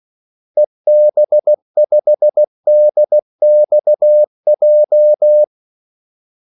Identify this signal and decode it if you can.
Morse code, heard as a keyed tone, about 16 wpm, 600 Hz. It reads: EB5DXJ
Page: Kaldesignaler 16 wpm